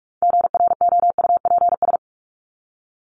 Morse code, heard as a keyed tone, about 44 wpm, 700 Hz. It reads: GROUPS
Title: ES:S Suffix - Top 100 Words with Base Words 45wpm